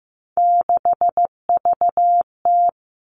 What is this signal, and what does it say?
Morse code, heard as a keyed tone, about 15 wpm, 700 Hz. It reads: 6VT
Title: Three Challenging Letter-Number Combinations 15wpm